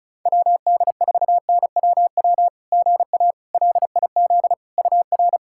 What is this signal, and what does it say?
Morse code, heard as a keyed tone, about 36 wpm, 690 Hz. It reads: WD4NWW GA LIZ UR
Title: SOTA QSO Practice 35wpm